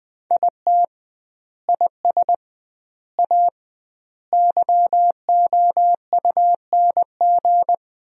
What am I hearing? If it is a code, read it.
Morse code, heard as a keyed tone, about 21 wpm, 700 Hz. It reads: IT IS A YOUNG
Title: 2x Word-Spacing - Sentences from Top 400 Words - 20wpm